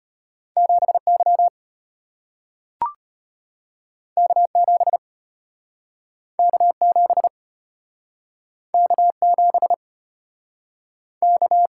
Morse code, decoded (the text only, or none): 7Y K7 K7 K7 K